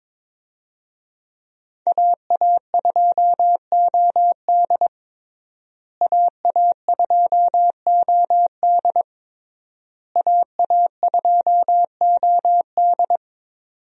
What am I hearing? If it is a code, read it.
Morse code, heard as a keyed tone, about 22 wpm, 700 Hz. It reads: AA2OD AA2OD AA2OD
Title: USA Call Signs - 5 Characters - Repeated 3x - 22wpm